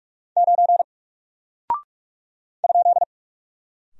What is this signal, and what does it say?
Morse code, heard as a keyed tone, about 46 wpm, 700 Hz. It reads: 9 ?
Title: Single Character (Letter:Number:Punctuation:Prosign) Alt 45wpm